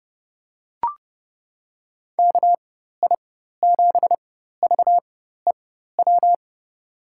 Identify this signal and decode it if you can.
Morse code, heard as a keyed tone, about 31 wpm, spaced as for 14 wpm, 700 Hz. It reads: KI7VEW
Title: USA Call Signs - 4x Character Spacing - 30wpm